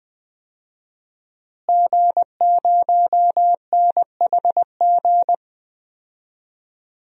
Morse code, heard as a keyed tone, about 20 wpm, 700 Hz. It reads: G0NHG
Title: DXCC-100 Call Signs - 5 Characters - Repeated 2x - 20wpm